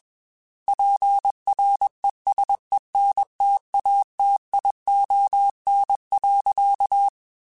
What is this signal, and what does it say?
Morse code, heard as a keyed tone, about 21 wpm, 800 Hz. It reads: PRESENTATION.
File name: Quotes / Quotes-mp3s / Persperation @20